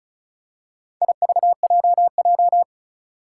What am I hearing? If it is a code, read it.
Morse code, heard as a keyed tone, about 34 wpm, 700 Hz. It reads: IVJJ
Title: Four Challenging Letter-Number Combinations 35wpm